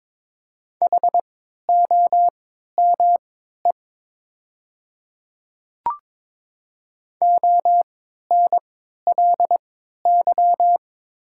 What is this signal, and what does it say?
Morse code, heard as a keyed tone, about 22 wpm, spaced as for 12 wpm, 700 Hz. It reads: HOME ONLY
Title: 3x Character Spacing - Top 300 Words - 22wpm